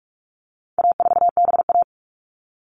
Morse code, heard as a keyed tone, about 44 wpm, 700 Hz. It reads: A4BA
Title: All Cut Numbers - 4-Digit 45wpm